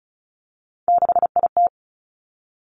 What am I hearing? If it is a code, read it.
Morse code, heard as a keyed tone, about 36 wpm, 700 Hz. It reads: 6IT
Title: Three Challenging Letter-Number Combinations 35wpm